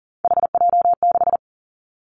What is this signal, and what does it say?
Morse code, heard as a keyed tone, about 42 wpm, 700 Hz. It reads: HJ6
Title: Callsign Prefixes for Top 100 DXCC - 40wpm